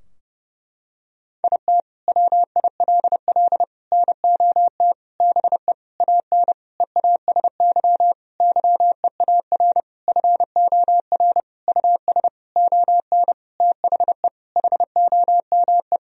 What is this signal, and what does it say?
Morse code, heard as a keyed tone, about 32 wpm, 700 Hz. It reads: IT WILL NOT BE AN EASY YEAR FOR US ON THE HOME
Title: Sentences from Top 1000 Words - Review Part 2 - 30wpm